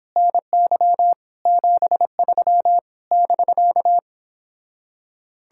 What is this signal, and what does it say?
Morse code, heard as a keyed tone, about 29 wpm, 700 Hz. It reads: NY 73 <BK>